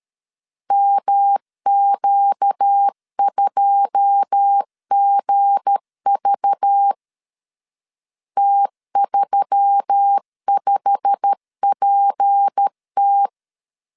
Morse code, heard as a keyed tone, about 13 wpm, 800 Hz. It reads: MQ2GV T35PT
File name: CW / DET / 1 9 1500C S B 12wpm 3819